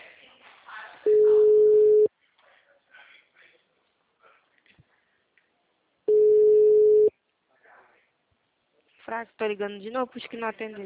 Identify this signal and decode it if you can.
Ringback tone (Europe, 425 Hz), with speech.
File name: out- -2010- - - .44111